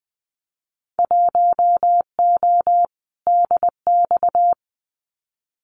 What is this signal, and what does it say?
Morse code, heard as a keyed tone, about 20 wpm, 700 Hz. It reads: 1O DX
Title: Winter Field Day Exchanges 20wpm